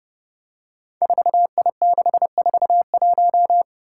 Morse code, heard as a keyed tone, about 30 wpm, 700 Hz. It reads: 4I641
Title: Five Challenging Letter-Number Combinations 30wpm